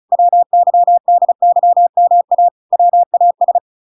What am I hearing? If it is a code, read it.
Morse code, heard as a keyed tone, about 34 wpm, 700 Hz. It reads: WYDYMA WAS